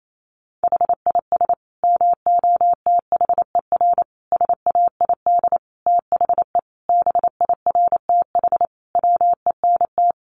Morse code, heard as a keyed tone, about 28 wpm, 700 Hz. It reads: HIS MOTHER SAID THE BIRTH WENT
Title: Sentences from Top 800 Words 28wpm